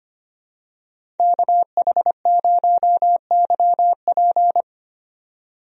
Morse code, heard as a keyed tone, about 25 wpm, 700 Hz. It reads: KH0YP
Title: USA Call Signs - 5 Characters - 25wpm